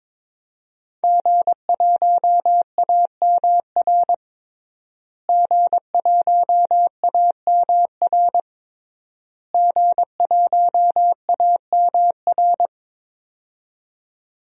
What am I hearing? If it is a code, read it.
Morse code, heard as a keyed tone, about 22 wpm, 700 Hz. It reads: G1AMR G1AMR G1AMR
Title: DXCC-100 Call Signs - 5 Characters - Repeated 3x - 22wpm